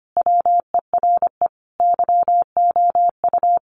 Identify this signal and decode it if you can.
Morse code, heard as a keyed tone, about 25 wpm, 700 Hz. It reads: WERE YOU
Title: Sentences from Top 1000 Words 25wpm